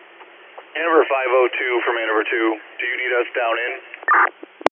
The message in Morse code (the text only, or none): none